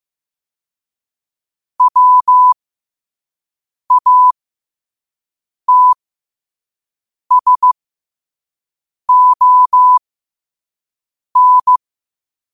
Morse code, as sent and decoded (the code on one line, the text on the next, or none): .-- .- - ... --- -.
WATSON